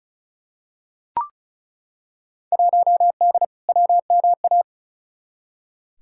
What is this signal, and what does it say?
Morse code, heard as a keyed tone, about 36 wpm, 700 Hz. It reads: 1D WMA
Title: Field Day Exchanges - 35wpm